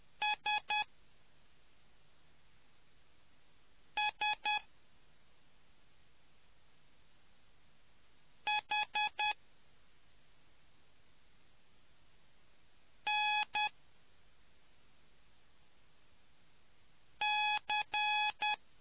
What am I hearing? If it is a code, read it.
Morse code, heard as a keyed tone, about 10 wpm, spaced as for 2 wpm, 840 Hz. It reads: SSHNC